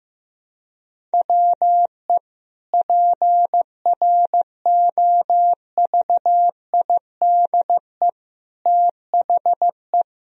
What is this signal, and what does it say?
Morse code, heard as a keyed tone, about 15 wpm, 700 Hz. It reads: WE PROVIDE THE
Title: Sentences from Top 1000 Words - Review Part 4 - 15wpm